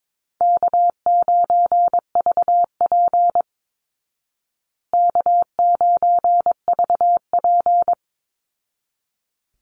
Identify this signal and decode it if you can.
Morse code, heard as a keyed tone, about 22 wpm, 690 Hz. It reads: K9VP K9VP
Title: Easy Call Signs Repeated 2x - 22wpm